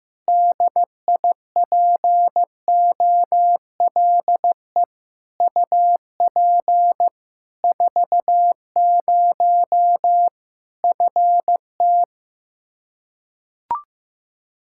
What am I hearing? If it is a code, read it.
Morse code, heard as a keyed tone, about 15 wpm, 700 Hz. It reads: DIPOLE UP 40 FT
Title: K1USN SST QSO Practice 15wpm